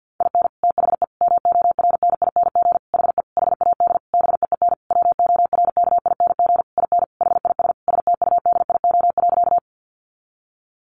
Morse code, heard as a keyed tone, about 50 wpm, 700 Hz. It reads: IN THE MORNING HE HAD BEEN WORKING IN HIS STUDIO.
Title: Sentences from Top 1200 Words 50wpm